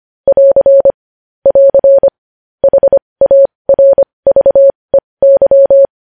Morse code, heard as a keyed tone, about 25 wpm, spaced as for 20 wpm, 550 Hz. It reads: <AR> <AR> HARVEY
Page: News Headlines in Morse Code at 20 WPM